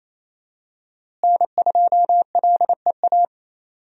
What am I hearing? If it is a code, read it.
Morse code, heard as a keyed tone, about 27 wpm, 700 Hz. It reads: N2LEA
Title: USA Call Signs - 5 Characters - Repeated 2x - 28wpm